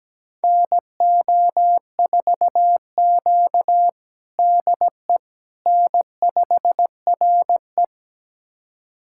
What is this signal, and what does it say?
Morse code, heard as a keyed tone, about 17 wpm, 700 Hz. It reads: NO4Q DE N5RE
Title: USA Callsign DE Callsign - 4 Characters - 17wpm